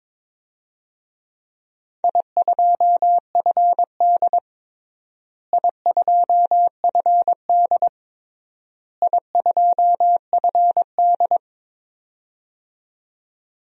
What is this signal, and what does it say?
Morse code, heard as a keyed tone, about 22 wpm, 700 Hz. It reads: I2FD I2FD I2FD
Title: DXCC-100 Call Signs - 4 Characters - Repeated 3x - 22wpm